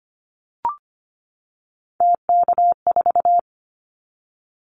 Morse code, heard as a keyed tone, about 25 wpm, 700 Hz. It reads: TK4
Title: Callsign Prefixes for Top 100 DXCC - 25wpm